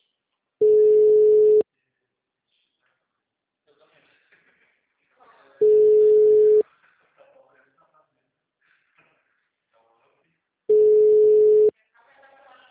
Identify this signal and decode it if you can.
Ringback tone (Europe, 425 Hz), with speech.
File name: out- -2024- - - .130